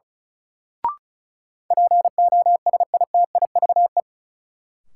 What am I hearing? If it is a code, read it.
Morse code, heard as a keyed tone, about 36 wpm, 700 Hz. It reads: POSITIVE
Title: Sentences from Top 1000 Words - Review Part 5 - New Word Drill - 35wpm